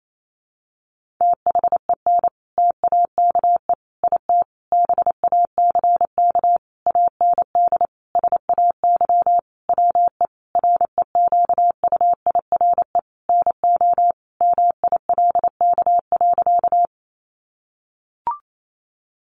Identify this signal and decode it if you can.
Morse code, heard as a keyed tone, about 28 wpm, 700 Hz. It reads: THEN TAKE IT BACK AND SAY WE REQUIRE NO MILK.
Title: Sentences from Top 1500 Words 28wpm